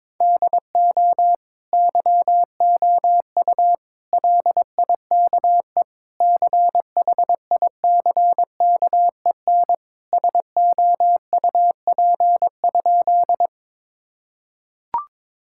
Morse code, heard as a keyed tone, about 22 wpm, 700 Hz. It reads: DO YOU LIKE CHICKEN SOUP?
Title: Sentences from Top 1000 Words - Review Part 6 - 22wpm